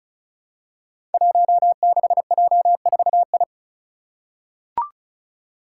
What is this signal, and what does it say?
Morse code, heard as a keyed tone, about 36 wpm, 700 Hz. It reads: 16J4I
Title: Five Challenging Letter-Number Combinations 35wpm